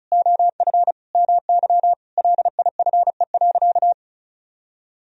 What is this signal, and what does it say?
Morse code, heard as a keyed tone, about 36 wpm, 700 Hz. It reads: OF MY LIFE.
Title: Sentences from Top 1000 Words - Review Part 1 - 35wpm